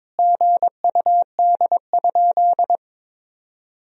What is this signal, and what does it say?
Morse code, heard as a keyed tone, about 22 wpm, 700 Hz. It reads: GUD?
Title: Sentences from Top 200 Words and Abbreviations 22wpm